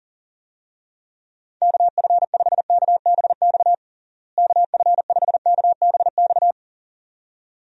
Morse code, heard as a keyed tone, about 40 wpm, 700 Hz. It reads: KF5KBX KF5KBX
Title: USA Call Signs - 6 Characters - Repeated 2x - 40wpm